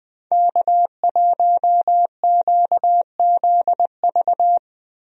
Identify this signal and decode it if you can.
Morse code, heard as a keyed tone, about 20 wpm, 700 Hz. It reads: K1QZV
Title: USA Call Signs - 5 Characters - Repeated 2x - 20wpm